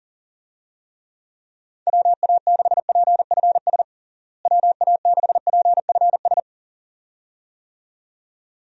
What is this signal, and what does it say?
Morse code, heard as a keyed tone, about 37 wpm, 700 Hz. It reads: WA6PFS WA6PFS
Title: USA Call Signs - 6 Characters - Repeated 2x - 40wpm